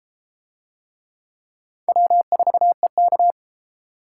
Morse code, heard as a keyed tone, about 32 wpm, 700 Hz. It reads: W4EK